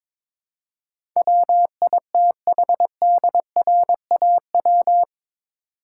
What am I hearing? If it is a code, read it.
Morse code, heard as a keyed tone, about 22 wpm, 700 Hz. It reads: WITHDRAW
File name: practice / 22wpm / Eight Letter Words - 22wpm